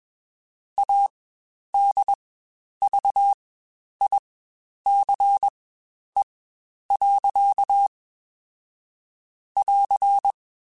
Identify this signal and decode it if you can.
Morse code, heard as a keyed tone, about 21 wpm, spaced as for 10 wpm, 800 Hz. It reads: ADVICE. <AR>
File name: Quotes Spaced / Quotes Spaced-mp3s / Advice sp @20